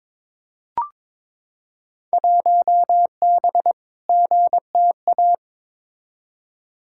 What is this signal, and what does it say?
Morse code, heard as a keyed tone, about 22 wpm, 700 Hz. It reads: 1B GTA
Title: Field Day Exchanges Encore - 22wpm